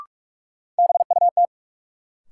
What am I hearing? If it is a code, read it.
Morse code, heard as a keyed tone, about 46 wpm, 700 Hz. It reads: BUT
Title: Sets of 2 Words - New Word Drill - 45wpm